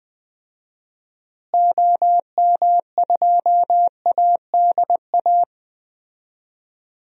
Morse code, heard as a keyed tone, about 20 wpm, 700 Hz. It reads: OM2ADA
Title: DXCC-100 Call Signs - 6 Characters - Repeated 3x - 20wpm